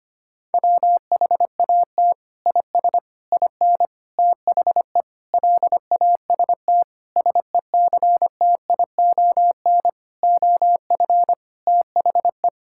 Morse code, heard as a keyed tone, about 25 wpm, 700 Hz. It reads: WHAT IS IN THE LAST SECTION OF THE
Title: Sentences from Top 1000 Words - Review Part 5 - 25wpm